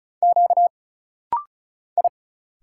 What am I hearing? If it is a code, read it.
Morse code, heard as a keyed tone, about 36 wpm, 700 Hz. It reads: Q I
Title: Single Letters - Rapid-Fire 35wpm